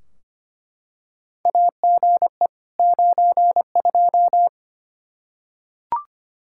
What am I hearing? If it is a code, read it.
Morse code, heard as a keyed tone, about 25 wpm, 700 Hz. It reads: AGE 92
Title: Practice QSO - Exchange 3 - More Info - 25wpm